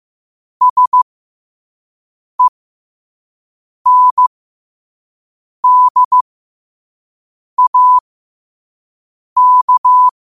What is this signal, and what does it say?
Morse code, heard as a keyed tone, about 15 wpm, spaced as for 5 wpm, 1000 Hz. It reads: SENDAK
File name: qotdfiles / 5wpm